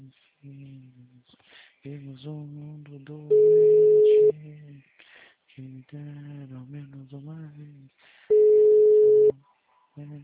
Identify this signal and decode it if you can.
Ringback tone (Europe, 425 Hz), with speech.